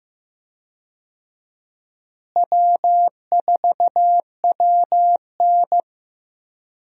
Morse code, heard as a keyed tone, about 15 wpm, 700 Hz. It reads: W4WN
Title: USA Call Signs - 4 Characters - Repeated 2x - 15wpm